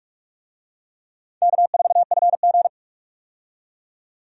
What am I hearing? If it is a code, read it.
Morse code, heard as a keyed tone, about 43 wpm, 700 Hz. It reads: K4FG
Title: USA Call Signs - 4 Characters - 45wpm